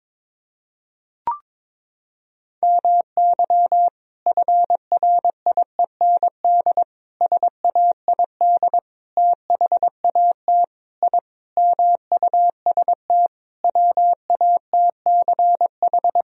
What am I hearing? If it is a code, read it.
Morse code, heard as a keyed tone, about 22 wpm, 700 Hz. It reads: MY FRIEND SAID THAT I MUST WATCH
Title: Sentences from Top 1000 Words - Review Part 4 - 22wpm